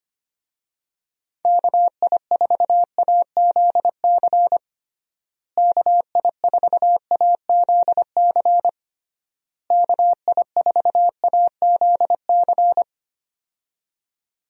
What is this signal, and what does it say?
Morse code, heard as a keyed tone, about 25 wpm, 700 Hz. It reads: KI4AZC KI4AZC KI4AZC
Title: USA Call Signs - 6 Characters - Repeated 3x - 25wpm